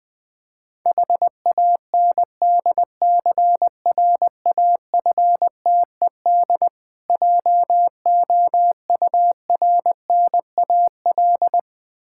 Morse code, heard as a keyed tone, about 19 wpm, 700 Hz. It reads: HANDCRAFTED JOURNAL
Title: Difficult Words with Hints - 20wpm